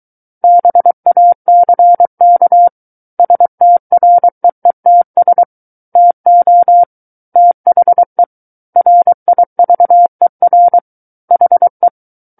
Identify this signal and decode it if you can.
Morse code, heard as a keyed tone, about 23 wpm, 700 Hz. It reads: BACK STREETS TO THE RIVER HE
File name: Book One / 14-Chapter 14 - 23 WPM